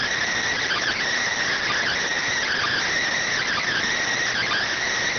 Emergency Alert System (SAME) tones, heard as harsh buzzing bursts. It is heard from 0.6 to 4.7 s.